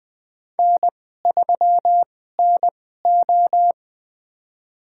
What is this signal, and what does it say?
Morse code, heard as a keyed tone, about 20 wpm, spaced as for 15 wpm, 700 Hz. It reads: N3NO
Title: USA Call Signs - 4 Characters - 2x Character Spacing - 20wpm